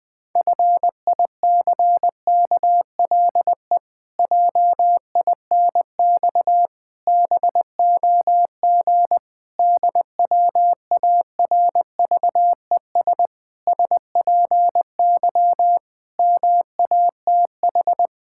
Morse code, heard as a keyed tone, about 20 wpm, 700 Hz. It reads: FICKLE JINX BOG DWARVES SPY MATH
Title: Pangram Challenge v2 - 20wpm